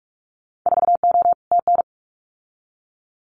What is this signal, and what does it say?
Morse code, heard as a keyed tone, about 44 wpm, 690 Hz. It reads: 4O TN
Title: Winter Field Day Exchanges Encore 45wpm